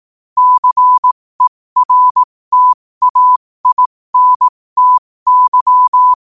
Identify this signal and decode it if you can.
Morse code, heard as a keyed tone, about 18 wpm, spaced as for 15 wpm, 1000 Hz. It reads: CERTAINTY
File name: qotdfiles / 15wpm